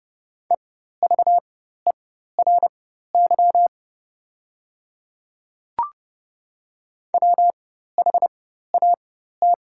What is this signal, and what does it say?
Morse code, heard as a keyed tone, about 30 wpm, spaced as for 14 wpm, 700 Hz. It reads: EVERY WHAT
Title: 4x Character Spacing - Top 200 Words - 30wpm